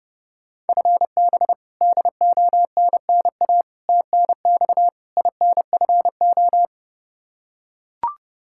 Morse code, heard as a keyed tone, about 30 wpm, 700 Hz. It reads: FB DONNA TNX INFO
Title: Practice QSO - Exchange 3 - More Info - 30wpm